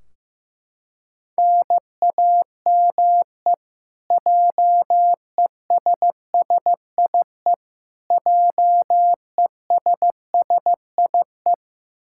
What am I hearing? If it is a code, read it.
Morse code, heard as a keyed tone, about 15 wpm, 700 Hz. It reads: NAME JESSIE JESSIE
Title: Practice QSO - Exchange 1 - Beginning - 15wpm